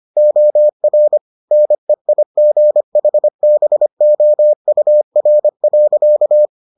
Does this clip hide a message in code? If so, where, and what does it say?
Yes: Morse code from 0.2 to 6.5 s, heard as a keyed tone, about 25 wpm, 600 Hz. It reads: OR NEIGHBOUR.